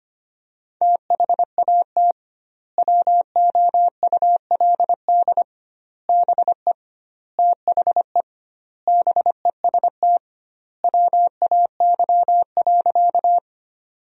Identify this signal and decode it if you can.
Morse code, heard as a keyed tone, about 25 wpm, 700 Hz. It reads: THAT WOULD BE THE BEST WAY.
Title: 2x Word-Spacing - Sentences from Top 400 Words - 25wpm